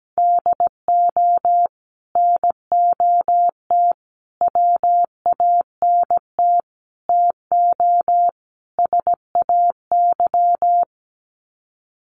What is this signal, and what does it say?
Morse code, heard as a keyed tone, about 17 wpm, 700 Hz. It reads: DO NOT WANT TO SAY
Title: Sets of 5 Words Encore 17wpm